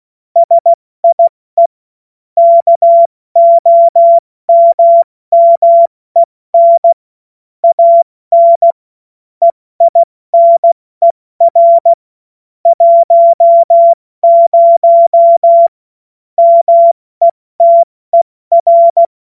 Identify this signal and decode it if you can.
Morse code, heard as a keyed tone, about 16 wpm, 680 Hz. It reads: SIE KOMMEN AN EINER 10 METER
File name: Morsekurs / Witze / Witz 30. Dezember 2025 70 F80